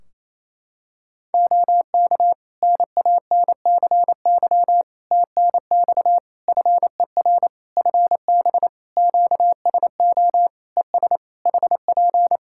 Morse code, heard as a keyed tone, about 28 wpm, 700 Hz. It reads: OK NANCY TNX FER FB QSO ES HP
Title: Practice QSO - Exchange 4 - Ending - 28wpm